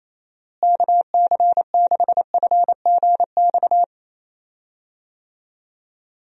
Morse code, heard as a keyed tone, about 28 wpm, 700 Hz. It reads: KC6FGX
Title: USA Call Signs - 6 Characters - Repeated 3x - 28wpm